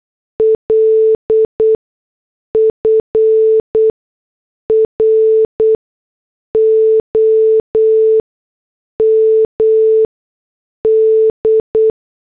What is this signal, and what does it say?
Morse code, heard as a keyed tone, about 8 wpm, spaced as for 6 wpm, 440 Hz. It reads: L FROM D
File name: morse / headlines